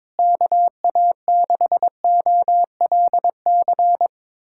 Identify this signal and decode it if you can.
Morse code, heard as a keyed tone, about 22 wpm, 700 Hz. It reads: KA6OLC
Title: USA Callsign DE Callsign - 6 Characters - 22wpm